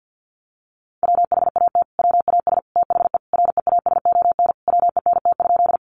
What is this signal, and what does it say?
Morse code, heard as a keyed tone, about 50 wpm, 700 Hz. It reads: WHAT WAS THE REASON WENT?
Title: Sentences from Top 700 Words 50wpm